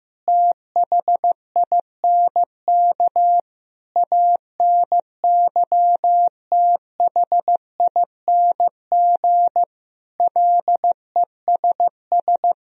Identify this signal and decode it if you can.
Morse code, heard as a keyed tone, about 16 wpm, 700 Hz. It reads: THINK ANYTHING LESS
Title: Sentences from Top 1000 Words - Review Part 2 - 15wpm